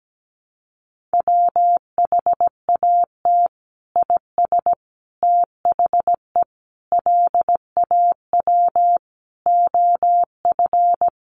Morse code, heard as a keyed tone, about 17 wpm, 700 Hz. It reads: WHAT IS THE LAW OF
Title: Sentences from Top 1300 Words 17wpm